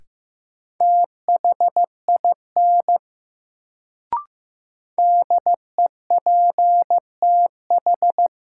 Morse code, heard as a keyed tone, about 15 wpm, 700 Hz. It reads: THIN DEPTH
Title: Sentences from Top 1200 Words - New Word Drill - 15wpm